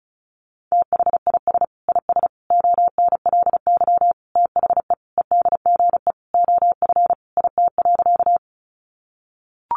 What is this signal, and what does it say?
Morse code, heard as a keyed tone, about 34 wpm, 700 Hz. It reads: THIS IS ONLY THE EDGE OF IT.
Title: Sentences from Top 1300 Words 35wpm